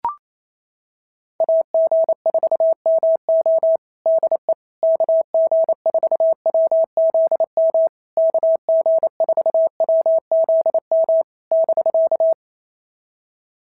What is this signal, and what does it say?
Morse code, heard as a keyed tone, about 28 wpm, 650 Hz. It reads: AG4MO DE KG4WZM KG4WZM <BK>
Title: SOTA QSO Practice 28wpm